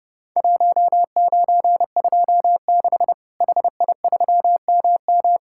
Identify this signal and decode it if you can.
Morse code, heard as a keyed tone, about 30 wpm, 700 Hz. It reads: 1926 HI3MM
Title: Worked All Europe DX Contest - QTC Exchange - 30wpm